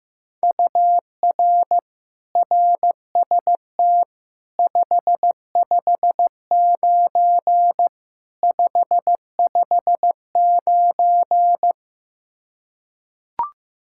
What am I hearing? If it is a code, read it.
Morse code, heard as a keyed tone, about 15 wpm, 700 Hz. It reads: UR RST 559 559
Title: Practice QSO - Exchange 1 - Beginning - 15wpm